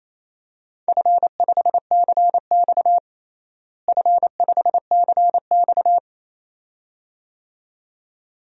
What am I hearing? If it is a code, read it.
Morse code, heard as a keyed tone, about 28 wpm, 700 Hz. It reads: F5CX F5CX
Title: DXCC-100 Call Signs - 4 Characters - Repeated 2x - 28wpm